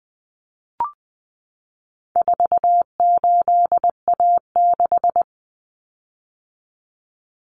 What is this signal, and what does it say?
Morse code, heard as a keyed tone, about 20 wpm, 700 Hz. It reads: 48A6
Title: Common Cut Numbers - 4-Digit 20wpm